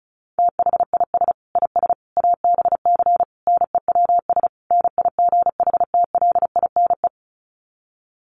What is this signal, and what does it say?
Morse code, heard as a keyed tone, about 34 wpm, 700 Hz. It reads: THIS IS ABC NEWS NIGHTLINE
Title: Sets of 5 Words 35wpm